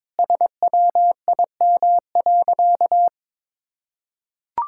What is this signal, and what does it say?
Morse code, heard as a keyed tone, about 22 wpm, 700 Hz. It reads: SWIM.
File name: practice / 22wpm / Sentences with 4-Word Phrases - 22wpm